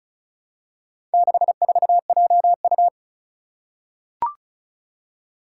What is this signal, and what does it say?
Morse code, heard as a keyed tone, about 34 wpm, 700 Hz. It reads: 64JU
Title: Four Challenging Letter-Number Combinations 35wpm